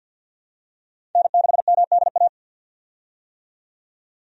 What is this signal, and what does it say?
Morse code, heard as a keyed tone, about 49 wpm, 700 Hz. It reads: N6MDA
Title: USA Call Signs - 5 Characters - 50wpm